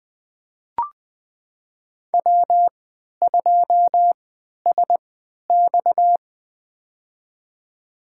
Morse code, heard as a keyed tone, about 20 wpm, spaced as for 11 wpm, 700 Hz. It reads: W2SX
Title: USA Call Signs - 4 Characters - 3x Character Spacing - 20wpm